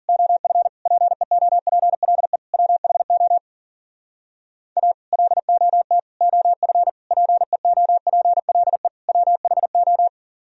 Morse code, heard as a keyed tone, about 51 wpm, 700 Hz. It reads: OF PEOPLE WHO A LOT OF PEOPLE WHO